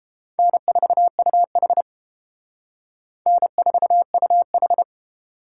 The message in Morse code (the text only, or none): N4UH N4UH